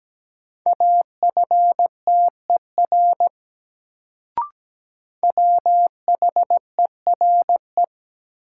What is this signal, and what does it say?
Morse code, heard as a keyed tone, about 17 wpm, 700 Hz. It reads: AFTER WHERE
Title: Sentences from Top 200 Words - New Word Drill - 17wpm